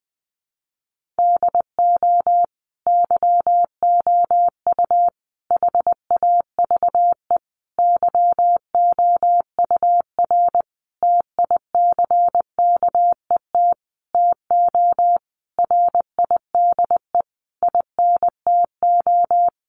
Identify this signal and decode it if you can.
Morse code, heard as a keyed tone, about 20 wpm, 700 Hz. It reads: DO YOU HAVE YOUR TICKET TO RIDE INTO
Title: Sentences from Top 1400 Words 20wpm